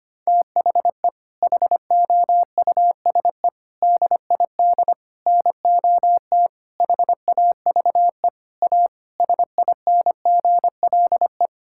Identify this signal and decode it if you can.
Morse code, heard as a keyed tone, about 25 wpm, 700 Hz. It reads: THE HOUSE DID NOT HAVE A SINGLE
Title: Sentences from Top 1600 Words 25wpm